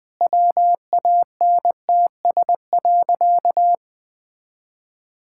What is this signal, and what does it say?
Morse code, heard as a keyed tone, about 21 wpm, 700 Hz. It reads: WANTS.
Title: Sentences with 2-Word Phrases - 20wpm